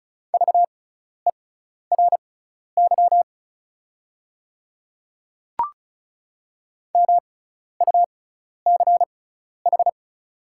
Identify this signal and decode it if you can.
Morse code, heard as a keyed tone, about 34 wpm, spaced as for 12 wpm, 690 Hz. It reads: VERY MUCH
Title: 6x Character Spacing - Top 200 Words - 35wpm